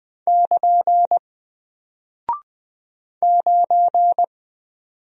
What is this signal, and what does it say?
Morse code, heard as a keyed tone, about 20 wpm, 700 Hz. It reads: <KN> 9
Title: Single Character (Letter:Number:Punctuation:Prosign) Alt 20wpm